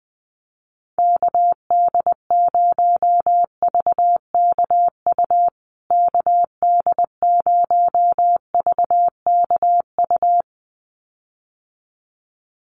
Morse code, heard as a keyed tone, about 20 wpm, 700 Hz. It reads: KD0VKU KD0VKU
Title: USA Call Signs Repeated 2x 20wpm